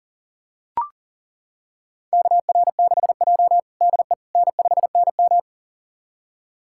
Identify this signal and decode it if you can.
Morse code, heard as a keyed tone, about 37 wpm, 700 Hz. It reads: KR6J DE N5NM
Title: USA Callsign DE Callsign - 4 Characters - 40wpm